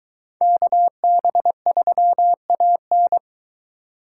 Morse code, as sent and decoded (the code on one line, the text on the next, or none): -.- -... ...-- .- -.
KB3AN